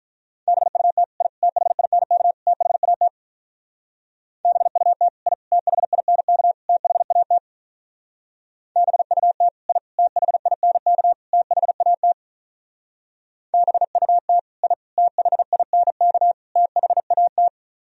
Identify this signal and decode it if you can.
Morse code, heard as a keyed tone, about 42 wpm, 700 Hz. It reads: BUT I THINK THAT BUT I THINK THAT BUT I THINK THAT BUT I THINK THAT
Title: Speed-Racing - Sets of 4 Words 35wpm